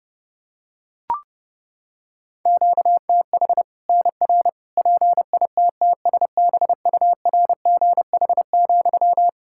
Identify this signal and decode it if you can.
Morse code, heard as a keyed tone, about 30 wpm, 700 Hz. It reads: QTH NR PITTSBURGH,
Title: Practice QSO - Exchange 1 - Beginning - 30wpm